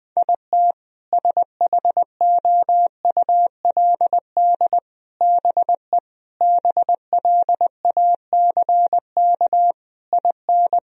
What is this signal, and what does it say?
Morse code, heard as a keyed tone, about 20 wpm, 700 Hz. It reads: IT SHOULD BE BLACK IN
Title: Sentences from Top 1000 Words - Review Part 2 - 20wpm